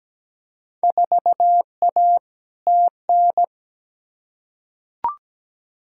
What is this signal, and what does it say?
Morse code, heard as a keyed tone, about 17 wpm, 700 Hz. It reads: 4A TN
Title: Field Day Exchanges Encore - 17wpm